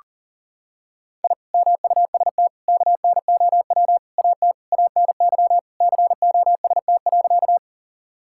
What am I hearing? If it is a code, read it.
Morse code, heard as a keyed tone, about 37 wpm, 700 Hz. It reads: I MUST KNOW AT ANY COST.
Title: Sentences from Top 1000 Words - Review Part 5 - 40wpm